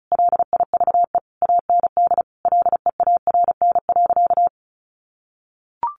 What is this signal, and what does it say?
Morse code, heard as a keyed tone, about 34 wpm, 700 Hz. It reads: LIVE AND LEARN.
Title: Sentences from Top 800 Words 35wpm